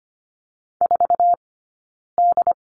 Morse code, heard as a keyed tone, about 25 wpm, 700 Hz. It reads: 4 D
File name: Lesson 17d - Letter u - Two Characters 25wpm